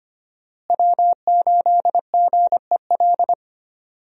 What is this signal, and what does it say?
Morse code, heard as a keyed tone, about 25 wpm, 700 Hz. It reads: W8GEL
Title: USA Call Signs - 5 Characters - Repeated 2x - 25wpm